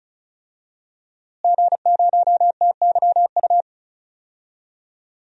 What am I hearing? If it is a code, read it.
Morse code, heard as a keyed tone, about 36 wpm, 700 Hz. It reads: G0TYU
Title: DXCC-100 Call Signs - 5 Characters - Repeated 2x - 35wpm